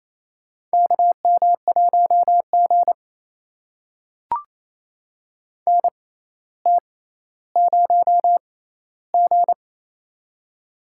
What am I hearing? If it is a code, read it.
Morse code, heard as a keyed tone, about 28 wpm, spaced as for 14 wpm, 700 Hz. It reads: KM1G NT0G
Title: USA Call Signs - 4 Characters - 6x Character Spacing - 28wpm